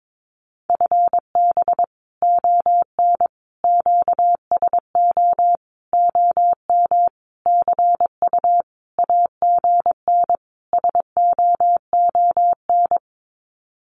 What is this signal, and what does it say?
Morse code, heard as a keyed tone, about 22 wpm, 700 Hz. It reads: FB ON QSO OM CU AGN SOON
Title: Common QSO Phrases 22wpm